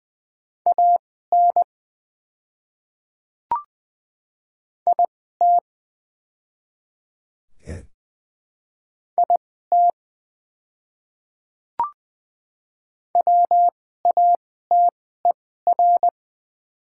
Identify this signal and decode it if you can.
Morse code, heard as a keyed tone, about 20 wpm, spaced as for 15 wpm, 690 Hz. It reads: AN IT IT WATER
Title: 2x Character Spacing - Top 100 Words - 20wpm